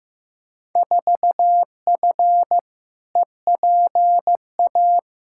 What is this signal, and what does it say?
Morse code, heard as a keyed tone, about 15 wpm, 700 Hz. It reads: 4F EPA
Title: Field Day Exchanges - 15wpm